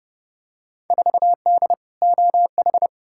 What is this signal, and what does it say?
Morse code, heard as a keyed tone, about 30 wpm, 700 Hz. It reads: 4D OH
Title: Field Day Exchanges Encore - 30wpm